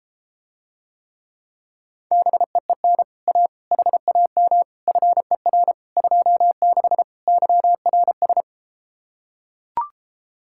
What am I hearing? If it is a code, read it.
Morse code, heard as a keyed tone, about 33 wpm, 700 Hz. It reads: BEEN A HAM FER 26 YRS